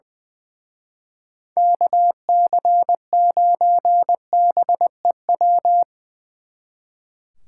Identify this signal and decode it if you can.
Morse code, heard as a keyed tone, about 22 wpm, 700 Hz. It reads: KC9BEW
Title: Speed Racing - USA 2x3 Call Signs 20wpm